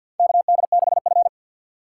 Morse code, heard as a keyed tone, about 52 wpm, 700 Hz. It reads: KD6F
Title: USA Call Signs - 4 Characters - 6x Character Spacing - 50wpm